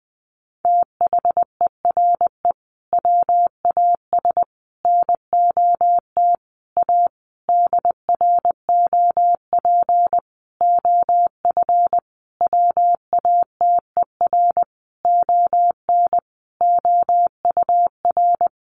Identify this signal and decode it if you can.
Morse code, heard as a keyed tone, about 20 wpm, 700 Hz. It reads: THERE WAS NOT A DROP OF WATER ON OUR
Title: Sentences from Top 1400 Words 20wpm